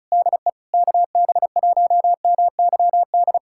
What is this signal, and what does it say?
Morse code, heard as a keyed tone, about 36 wpm, 700 Hz. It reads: DE KB1MYD
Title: USA Callsign DE Callsign - 6 Characters - 35wpm